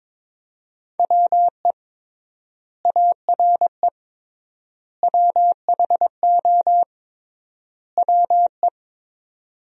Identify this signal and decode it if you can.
Morse code, heard as a keyed tone, about 22 wpm, 700 Hz. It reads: WE ARE WHO WE
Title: 3x Word-Spacing - Sentences from Top 200 Words - 22wpm